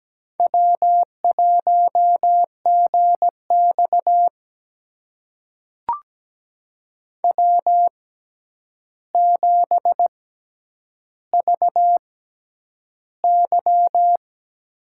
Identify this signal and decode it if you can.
Morse code, heard as a keyed tone, about 17 wpm, spaced as for 9 wpm, 700 Hz. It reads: W1GX W7VY
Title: USA Call Signs - 4 Characters - 6x Character Spacing - 17wpm